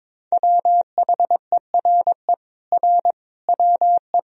Morse code, heard as a keyed tone, about 22 wpm, 700 Hz. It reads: WHERE R WE
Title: Sentences from Top 200 Words and Abbreviations 22wpm